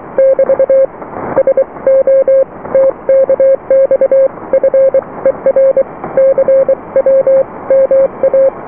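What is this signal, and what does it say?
Morse code, heard as a keyed tone, about 23 wpm, 540 Hz. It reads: <BT> SO TKX FER CW MA
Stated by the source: Morse QSO on 14 MHz with a German station, received at RST 579.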